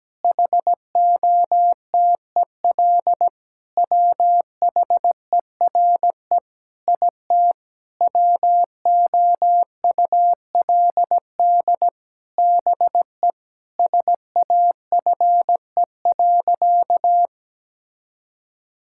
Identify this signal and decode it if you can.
Morse code, heard as a keyed tone, about 17 wpm, 700 Hz. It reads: HOTEL WHERE IT WOULD BE SAFE.
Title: Sentences from Top 1600 Words 17wpm